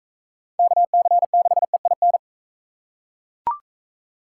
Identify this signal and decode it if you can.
Morse code, heard as a keyed tone, about 38 wpm, 700 Hz. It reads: KC6EIN